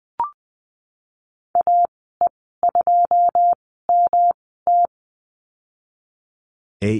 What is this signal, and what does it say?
Morse code, heard as a keyed tone, about 19 wpm, spaced as for 14 wpm, 700 Hz. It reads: AE2MT
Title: 140 Lesson 23g - Letter f - Call Signs 20wpm